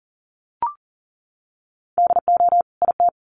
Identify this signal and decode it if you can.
Morse code, heard as a keyed tone, about 40 wpm, 700 Hz. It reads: DO IT
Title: Sets of 2 Words 40wpm